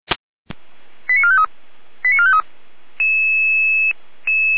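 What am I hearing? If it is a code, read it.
Five-tone selective call (ZVEI), heard as quick runs of five tones, tones of about 70 ms. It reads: 89453 89453
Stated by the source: ZVEI five-tone selective call received on an SDR.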